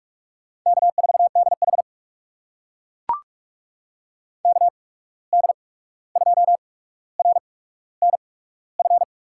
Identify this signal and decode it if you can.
Morse code, heard as a keyed tone, about 42 wpm, spaced as for 17 wpm, 700 Hz. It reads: K4DH KD2RNF
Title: USA Call Signs - 8x Character Spacing - 45wpm